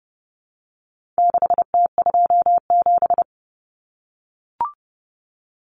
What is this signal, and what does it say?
Morse code, heard as a keyed tone, about 30 wpm, 700 Hz. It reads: 6T27
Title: Common Cut Numbers - 4-Digit 30wpm